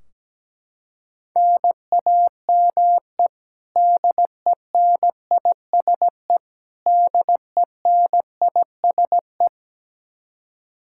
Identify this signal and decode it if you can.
Morse code, heard as a keyed tone, about 17 wpm, 700 Hz. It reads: NAME DENISE DENISE
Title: Practice QSO - Exchange 1 - Beginning - 17wpm